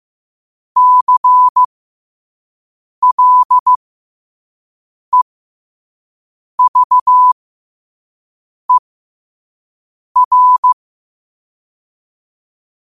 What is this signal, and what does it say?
Morse code, heard as a keyed tone, about 15 wpm, spaced as for 5 wpm, 1000 Hz. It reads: CLEVER
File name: qotdfiles / 5wpm